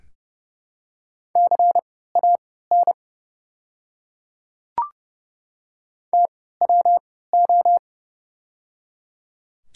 Morse code, heard as a keyed tone, about 30 wpm, spaced as for 17 wpm, 700 Hz. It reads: CAN TWO
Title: 3x Character Spacing - Top 100 Words - 30wpm